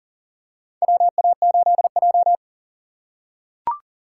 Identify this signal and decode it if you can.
Morse code, heard as a keyed tone, about 38 wpm, 700 Hz. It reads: WA8J
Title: USA Call Signs - 4 Characters - 3x Character Spacing - 40wpm